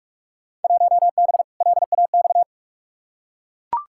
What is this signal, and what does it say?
Morse code, heard as a keyed tone, about 42 wpm, 700 Hz. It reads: 1B LAX
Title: Field Day Exchanges - 45wpm